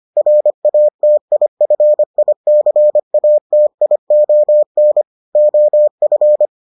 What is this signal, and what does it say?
Morse code, heard as a keyed tone, about 25 wpm, 600 Hz. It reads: RATIFICATION OF